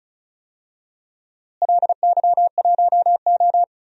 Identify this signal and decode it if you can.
Morse code, heard as a keyed tone, about 34 wpm, 700 Hz. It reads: LY1O
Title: DXCC-100 Call Signs - 4 Characters - Repeated 3x - 35wpm